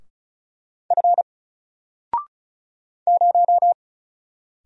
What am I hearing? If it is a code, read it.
Morse code, heard as a keyed tone, about 34 wpm, 700 Hz. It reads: F 0
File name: Single-Letter-Number-35wpm